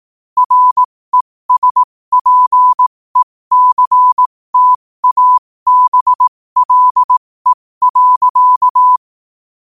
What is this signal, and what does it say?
Morse code, heard as a keyed tone, about 18 wpm, spaced as for 15 wpm, 1000 Hz. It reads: RESPECTABLE.
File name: qotdfiles / 15wpm